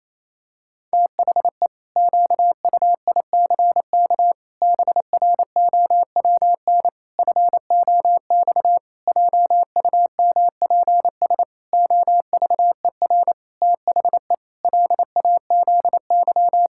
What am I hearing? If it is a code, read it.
Morse code, heard as a keyed tone, about 28 wpm, 700 Hz. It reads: THE QUICK BROWN FOX JUMPS OVER THE LAZY
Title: Pangram Challenge v2 - 28wpm